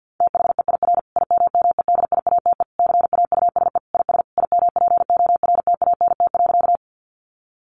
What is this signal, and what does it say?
Morse code, heard as a keyed tone, about 50 wpm, 700 Hz. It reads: THEIR IMMEDIATE CAUSE IS IMPORTANT.
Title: Sentences from Top 1200 Words 50wpm